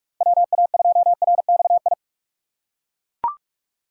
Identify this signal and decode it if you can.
Morse code, heard as a keyed tone, about 41 wpm, 700 Hz. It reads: WA2RXI
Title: USA Call Signs - 6 Characters - 4x Character Spacing - 45wpm